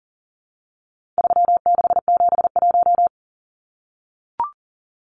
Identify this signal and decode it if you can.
Morse code, heard as a keyed tone, about 38 wpm, 700 Hz. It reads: 3671
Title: Numbers - 4-Digit 40wpm